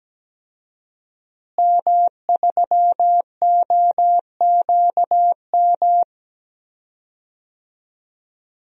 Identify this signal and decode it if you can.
Morse code, heard as a keyed tone, about 17 wpm, 700 Hz. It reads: M3OQM
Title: DXCC-100 Call Signs - 5 Characters - Repeated 3x - 17wpm